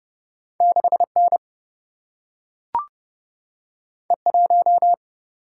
Morse code, heard as a keyed tone, about 30 wpm, 700 Hz. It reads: 6N E1
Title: Single-Letter Single-Number 30wpm